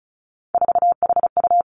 Morse code, heard as a keyed tone, about 36 wpm, 700 Hz. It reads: 4HU
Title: Three Challenging Letter-Number Combinations 35wpm